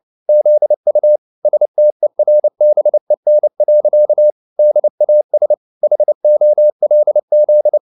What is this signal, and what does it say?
Morse code, heard as a keyed tone, about 29 wpm, 600 Hz. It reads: ZU STERBEN. DAS HOLZ